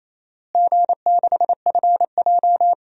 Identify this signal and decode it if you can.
Morse code, heard as a keyed tone, about 28 wpm, 700 Hz. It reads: G6FJ
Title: DXCC-100 Call Signs - 4 Characters - Repeated 3x - 28wpm